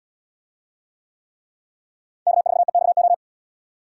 Morse code, heard as a keyed tone, about 71 wpm, 700 Hz. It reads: K4EXO